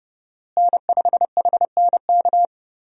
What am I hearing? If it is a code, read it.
Morse code, heard as a keyed tone, about 30 wpm, 700 Hz. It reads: N5HNK
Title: USA Call Signs - 5 Characters - Repeated 2x - 30wpm